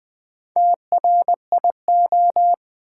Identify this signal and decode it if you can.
Morse code, heard as a keyed tone, about 20 wpm, 700 Hz. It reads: TRIO
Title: Build A Word - 2 Letters At A Time - 4-Letter Words - 20wpm